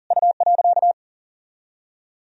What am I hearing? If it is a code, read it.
Morse code, heard as a keyed tone, about 40 wpm, 700 Hz. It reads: U.
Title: Sentences from Top 300 Words and Abbreviations 40wpm